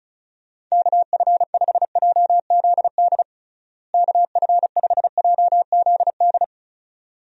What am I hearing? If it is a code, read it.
Morse code, heard as a keyed tone, about 34 wpm, 700 Hz. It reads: KF5JZD KF5JZD
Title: USA Call Signs - 6 Characters - Repeated 2x - 35wpm